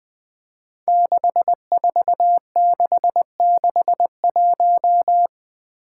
Morse code, heard as a keyed tone, about 20 wpm, 700 Hz. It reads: 64661
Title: Five Challenging Letter-Number Combinations 20wpm